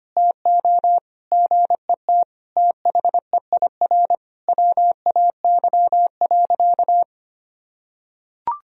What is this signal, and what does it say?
Morse code, heard as a keyed tone, about 25 wpm, 700 Hz. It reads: TO GET THEIR WAY.
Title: Sentences from Top 1000 Words - Review Part 4 - 25wpm